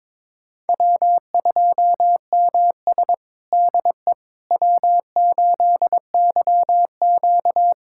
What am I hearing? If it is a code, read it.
Morse code, heard as a keyed tone, about 22 wpm, 700 Hz. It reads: W2MS DE W8YQ
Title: USA Callsign DE Callsign - 4 Characters - 22wpm